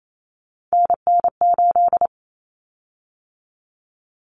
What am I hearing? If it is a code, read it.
Morse code, heard as a keyed tone, about 28 wpm, 700 Hz. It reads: NN8
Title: Common Cut Numbers - 3-Digit 28wpm